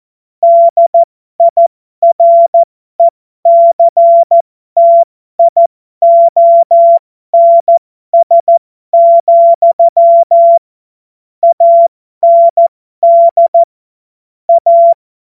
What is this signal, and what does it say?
Morse code, heard as a keyed tone, about 14 wpm, spaced as for 12 wpm, 680 Hz. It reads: DIRECTIONS, AND A
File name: Morsekurs / Witze